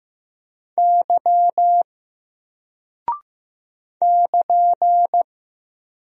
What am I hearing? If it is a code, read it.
Morse code, heard as a keyed tone, about 15 wpm, 700 Hz. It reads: Y <KN>
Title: Single Character (Letter:Number:Punctuation:Prosign) 15wpm